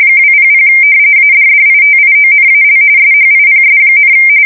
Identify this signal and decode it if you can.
RTTY, about 45.45 baud. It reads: 200 ADDITIONAL SHIPS,
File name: fsk170 / usnavy4